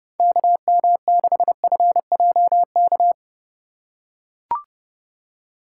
Morse code, heard as a keyed tone, about 31 wpm, 700 Hz. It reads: KM6FJK
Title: USA Call Signs - 6 Characters - 2x Character Spacing - 30wpm